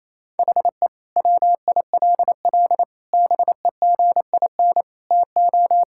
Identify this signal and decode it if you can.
Morse code, heard as a keyed tone, about 28 wpm, 700 Hz. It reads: HE WILL BEGIN TO
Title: Sentences from Top 1600 Words 28wpm